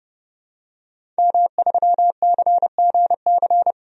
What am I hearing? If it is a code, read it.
Morse code, heard as a keyed tone, about 30 wpm, 700 Hz. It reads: M3CGC
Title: DXCC-100 Call Signs - 5 Characters - 30wpm